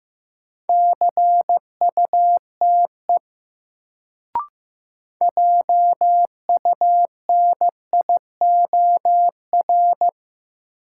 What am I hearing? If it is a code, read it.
Morse code, heard as a keyed tone, about 15 wpm, 700 Hz. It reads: CUTE JUNIOR
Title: Sentences from Top 1000 Words - Review Part 7 - New Word Drill - 15wpm